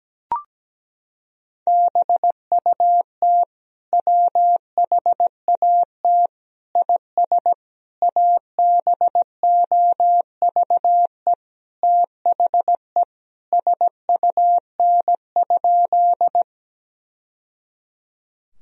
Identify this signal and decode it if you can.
Morse code, heard as a keyed tone, about 17 wpm, 700 Hz. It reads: BUT WHAT IS ABOVE THE SUN?
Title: Sentences from Top 1000 Words - Review Part 2 - 17wpm